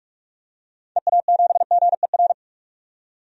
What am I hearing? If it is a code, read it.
Morse code, heard as a keyed tone, about 42 wpm, 700 Hz. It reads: EA7GER
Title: DXCC-100 Call Signs - 6 Characters - 45wpm